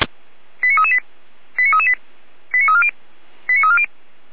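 Five-tone selective call (ZVEI), heard as quick runs of five tones, tones of about 70 ms. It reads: 89338 89338 89340 89340
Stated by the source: Five-tone ZVEI selective call 89325 received by SDR, 455 channel.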